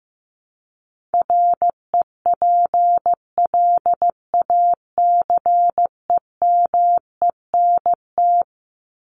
Morse code, heard as a keyed tone, about 15 wpm, 700 Hz. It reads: REPLACEMENT
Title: Top 1751-2000 Words 15wpm